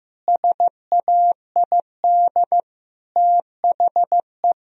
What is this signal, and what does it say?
Morse code, heard as a keyed tone, about 15 wpm, 700 Hz. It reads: SAID THE
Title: Sentences from Top 1000 Words - Review Part 1 - 15wpm